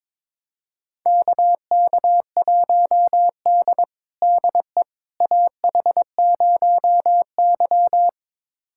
Morse code, heard as a keyed tone, about 22 wpm, 700 Hz. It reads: KK1D DE AH0Y
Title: USA Callsign DE Callsign - 4 Characters - 22wpm